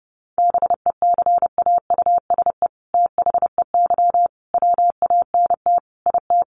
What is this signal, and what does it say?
Morse code, heard as a keyed tone, about 30 wpm, 700 Hz. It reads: BECAUSE THEY WANT IT
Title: Sentences from Top 800 Words 30wpm